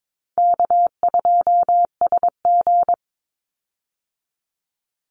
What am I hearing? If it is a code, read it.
Morse code, heard as a keyed tone, about 22 wpm, 700 Hz. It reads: K2SG
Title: Easy Call Signs - 22wpm